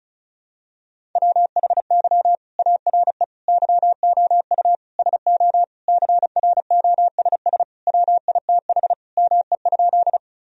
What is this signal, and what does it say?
Morse code, heard as a keyed tone, about 36 wpm, 700 Hz. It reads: WHY ARE YOU SO CROSS WITH ME?
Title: Sentences from Top 1000 Words - Review Part 1 - 35wpm